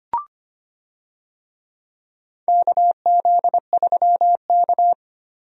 Morse code, heard as a keyed tone, about 25 wpm, 700 Hz. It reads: KZ3K
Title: USA Call Signs - 4 Characters - Repeated 3x - 25wpm